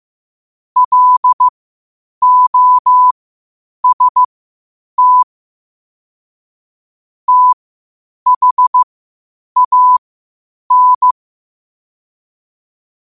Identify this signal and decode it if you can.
Morse code, heard as a keyed tone, about 15 wpm, spaced as for 8 wpm, 1000 Hz. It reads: LOST THAN
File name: qotdfiles / 8wpm